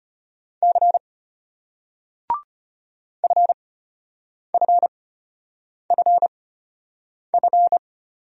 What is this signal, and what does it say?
Morse code, heard as a keyed tone, about 34 wpm, spaced as for 8 wpm, 700 Hz. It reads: C FFFF